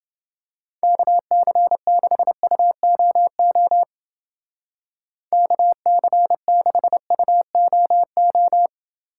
Morse code, heard as a keyed tone, about 28 wpm, 700 Hz. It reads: KC6UOO KC6UOO